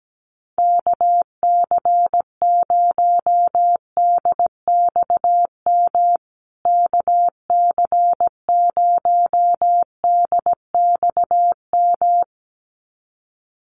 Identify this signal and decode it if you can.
Morse code, heard as a keyed tone, about 17 wpm, 700 Hz. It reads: KC0DXM KC0DXM
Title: USA Call Signs Repeated 2x 17wpm